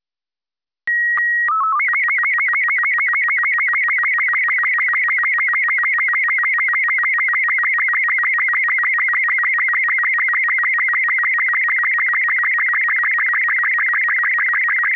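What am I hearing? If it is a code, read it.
SSTV (slow-scan television), Robot 36.
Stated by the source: SSTV Robot 36 mode.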